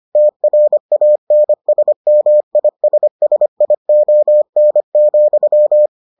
Morse code, heard as a keyed tone, about 25 wpm, 600 Hz. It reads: TRANSMISSION,